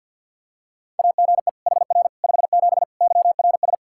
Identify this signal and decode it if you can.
Morse code, heard as a keyed tone, about 52 wpm, 700 Hz. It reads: AGE HR 57 YRS
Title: K1USN SST QSO Practice 50wpm